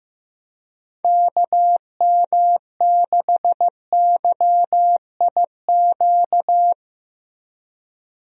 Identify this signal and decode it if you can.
Morse code, heard as a keyed tone, about 15 wpm, 700 Hz. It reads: KM6YIQ
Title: USA Call Signs - 6 Characters - Repeated 2x - 15wpm